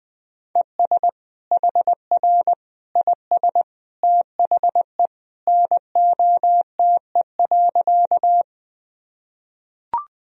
Morse code, heard as a keyed tone, about 21 wpm, 700 Hz. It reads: ES HR IS THE NOTE.
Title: Sentences from Top 500 Words and Abbreviations 20wpm